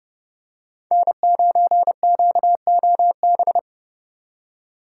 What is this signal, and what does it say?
Morse code, heard as a keyed tone, about 30 wpm, 700 Hz. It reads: N9QOB
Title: USA Call Signs - 2x Character Spacing - 30wpm